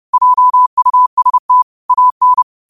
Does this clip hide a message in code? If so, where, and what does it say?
Yes: Morse code from 0.1 to 2.4 s, heard as a keyed tone, about 30 wpm, 1000 Hz. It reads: JUST AN